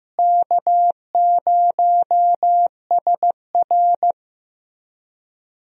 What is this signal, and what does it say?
Morse code, heard as a keyed tone, about 15 wpm, 700 Hz. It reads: K0SR
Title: USA Call Signs - 4 Characters - Repeated 2x - 15wpm